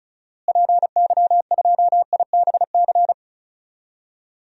Morse code, heard as a keyed tone, about 34 wpm, 700 Hz. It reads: PY2IBC
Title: DXCC-100 Call Signs - 6 Characters - Repeated 3x - 35wpm